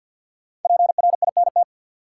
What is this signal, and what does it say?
Morse code, heard as a keyed tone, about 49 wpm, 700 Hz. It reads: PRINT
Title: Sentences from Top 1400 Words - New Word Drill Alt - 50wpm